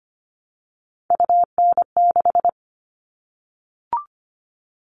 Morse code, heard as a keyed tone, about 25 wpm, 700 Hz. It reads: UN6
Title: All Cut Numbers - 3-Digit 25wpm